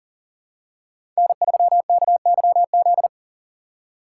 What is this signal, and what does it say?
Morse code, heard as a keyed tone, about 37 wpm, 700 Hz. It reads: N3KYZ
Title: USA Call Signs - 5 Characters - 6x Character Spacing - 40wpm